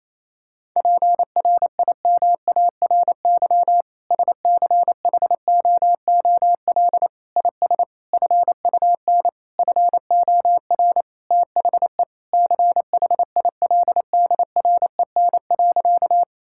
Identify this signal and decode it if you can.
Morse code, heard as a keyed tone, about 28 wpm, 700 Hz. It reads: PRIMARY SCHOOL IS FUN FOR THE CHILDREN.
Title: Sentences from Top 1000 Words - Review Part 5 - 28wpm